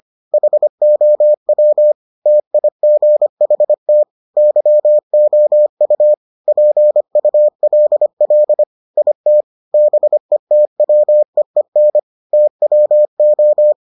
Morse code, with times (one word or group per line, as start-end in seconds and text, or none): 0.3-1.9 HOW
2.3-4.0 TIGHT
4.4-6.1 YOU
6.5-8.6 PULL
9.0-9.4 IT
9.7-12.0 BETWEEN
12.3-13.7 TWO